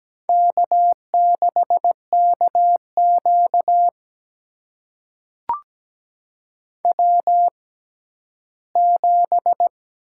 Morse code, heard as a keyed tone, about 17 wpm, spaced as for 12 wpm, 700 Hz. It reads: K6KQ W7
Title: USA Call Signs - 4 Characters - 6x Character Spacing - 17wpm